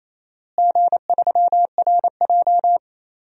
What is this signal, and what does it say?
Morse code, heard as a keyed tone, about 27 wpm, 700 Hz. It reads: G3RJ
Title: DXCC-100 Call Signs - 4 Characters - Repeated 2x - 28wpm